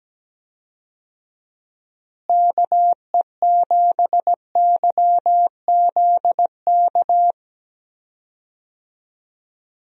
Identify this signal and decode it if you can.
Morse code, heard as a keyed tone, about 17 wpm, 700 Hz. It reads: KE7YZK
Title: USA Call Signs - 6 Characters - Repeated 2x - 17wpm